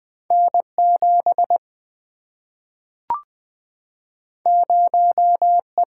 Morse code, heard as a keyed tone, about 20 wpm, 700 Hz. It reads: N 7 0 E
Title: Single-Letter Single-Number 20wpm